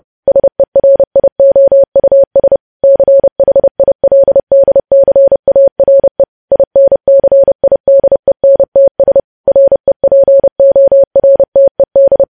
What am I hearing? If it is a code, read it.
Morse code, heard as a keyed tone, about 30 wpm, 550 Hz. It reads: SERIOUS CHILDCARE INCIDENTS REPORTED
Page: news headlines in morse code at 30 wpm